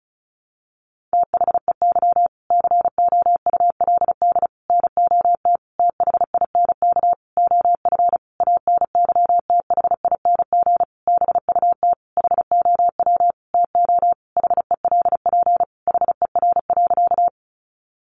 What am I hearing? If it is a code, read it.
Morse code, heard as a keyed tone, about 34 wpm, 700 Hz. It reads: THEY COULD NOT THINK OF ANYTHING BUT HOW TO HELP HER.
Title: Sentences from Top 1000 Words 35wpm